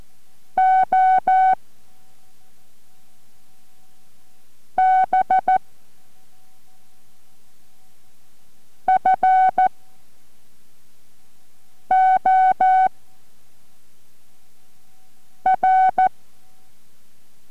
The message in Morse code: OB FOR